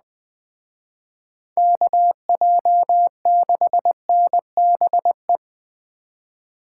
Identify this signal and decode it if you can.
Morse code, heard as a keyed tone, about 20 wpm, 700 Hz. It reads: KJ6NBE
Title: Speed Racing - USA 2x3 Call Signs 17wpm